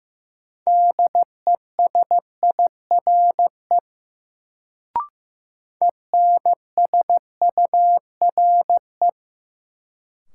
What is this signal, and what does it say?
Morse code, heard as a keyed tone, about 15 wpm, 700 Hz. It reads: DESIRE ENSURE
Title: Sentences from Top 1100 Words - New Word Drill Alt - 15wpm